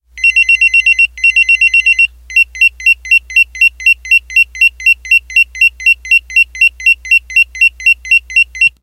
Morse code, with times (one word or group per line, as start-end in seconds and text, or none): none